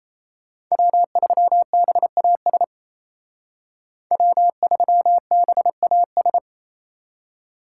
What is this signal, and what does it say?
Morse code, heard as a keyed tone, about 34 wpm, 700 Hz. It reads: W3BAS W3BAS
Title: Speed Racing - USA 1x3 Call Signs 28wpm